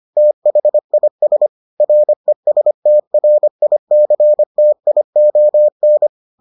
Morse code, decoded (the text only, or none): THIS RESTRICTION